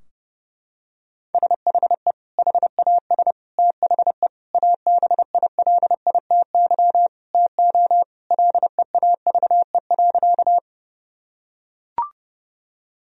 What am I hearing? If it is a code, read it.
Morse code, heard as a keyed tone, about 30 wpm, 700 Hz. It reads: SHE HAS THE ABILITY TO LEAVE.
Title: Sentences from Top 1000 Words - Review Part 3 - 30wpm